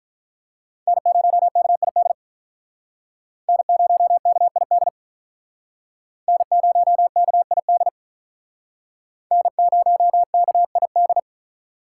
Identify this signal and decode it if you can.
Morse code, heard as a keyed tone, about 47 wpm, 700 Hz. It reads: N0KID N0KID N0KID N0KID